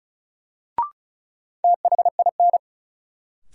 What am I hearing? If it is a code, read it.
Morse code, heard as a keyed tone, about 36 wpm, 700 Hz. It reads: THIN
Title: Sentences from Top 1200 Words - New Word Drill Alt - 35wpm